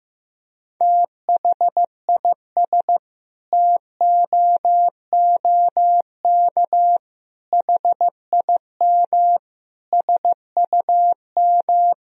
Morse code, heard as a keyed tone, about 15 wpm, 700 Hz. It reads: THIS TOOK HIM SUM
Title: Sentences from Top 300 Words and Abbreviations 15wpm